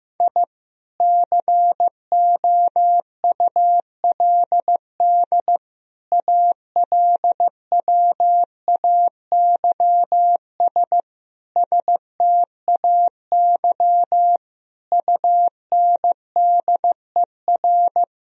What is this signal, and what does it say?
Morse code, heard as a keyed tone, about 15 wpm, 700 Hz. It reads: I COULD ALWAYS STAY UNDER
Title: Sentences from Top 1000 Words - Review Part 1 - 15wpm